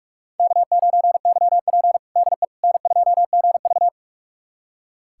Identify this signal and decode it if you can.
Morse code, heard as a keyed tone, about 43 wpm, 700 Hz. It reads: K9YP DE N2GV
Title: USA Callsign DE Callsign - 4 Characters - 45wpm